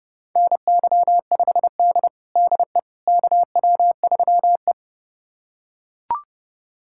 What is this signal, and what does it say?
Morse code, heard as a keyed tone, about 32 wpm, 700 Hz. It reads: NY5D DE KW3E
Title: USA Callsign DE Callsign - 4 Characters - 30wpm